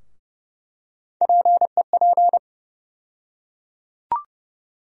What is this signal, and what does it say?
Morse code, heard as a keyed tone, about 30 wpm, 700 Hz. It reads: PEP
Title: USA Call Sign Postfixes - 3 Letters 30wpm